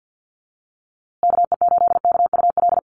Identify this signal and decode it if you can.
Morse code, heard as a keyed tone, about 49 wpm, 700 Hz. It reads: KE8KUL
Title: USA Call Signs Repeated 3x 50wpm